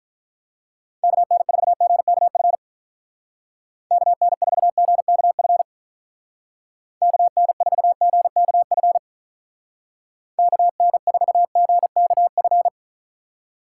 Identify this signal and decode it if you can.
Morse code, heard as a keyed tone, about 44 wpm, 700 Hz. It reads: KN4GKF KN4GKF KN4GKF KN4GKF
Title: Speed Racing - USA 2x3 Call Signs 35wpm